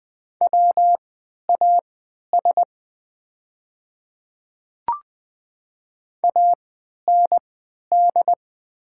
Morse code, heard as a keyed tone, about 20 wpm, spaced as for 11 wpm, 700 Hz. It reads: WAS AND
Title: 3x Character Spacing - Top 100 Words - 20wpm